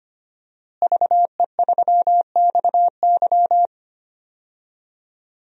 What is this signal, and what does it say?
Morse code, heard as a keyed tone, about 25 wpm, 700 Hz. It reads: VE3XY
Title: DXCC-100 Call Signs - 5 Characters - Repeated 3x - 25wpm